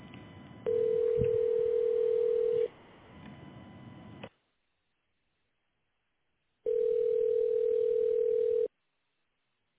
Ringback tone (North America), 440 + 480 Hz.